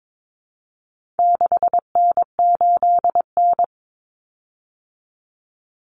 Morse code, heard as a keyed tone, about 22 wpm, 700 Hz. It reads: 6N8N
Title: Common Cut Numbers - 4-Digit 22wpm